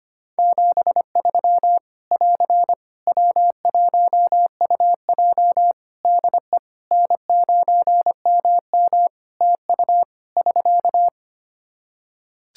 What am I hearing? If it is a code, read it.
Morse code, heard as a keyed tone, about 25 wpm, 700 Hz. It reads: 73 <AR> W1UJ DE N9MM TU <SK>
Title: Practice QSO - Exchange 4 - Ending - 25wpm